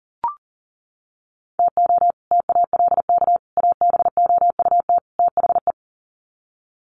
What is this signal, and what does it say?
Morse code, heard as a keyed tone, about 40 wpm, 700 Hz. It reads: TO TALK ABOUT THE
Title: Sets of 4 Words 40wpm